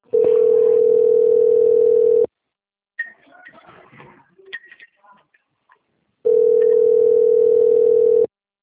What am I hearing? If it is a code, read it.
Ringback tone (North America), 440 + 480 Hz.